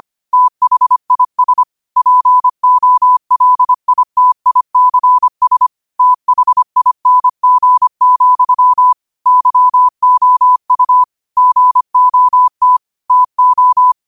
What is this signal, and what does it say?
Morse code, heard as a keyed tone, about 25 wpm, 1000 Hz. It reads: THIS POLITICS THING, YOU GOT TO
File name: qotdfiles / 25wpm